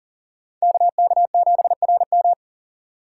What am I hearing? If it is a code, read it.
Morse code, heard as a keyed tone, about 40 wpm, 700 Hz. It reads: KK7RM
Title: USA Call Signs - 5 Characters - 3x Character Spacing - 40wpm